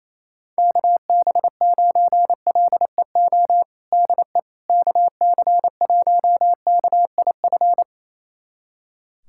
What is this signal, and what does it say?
Morse code, heard as a keyed tone, about 28 wpm, 700 Hz. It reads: KB9LEO DE KC1KIF
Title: USA Callsign DE Callsign - 6 Characters - 28wpm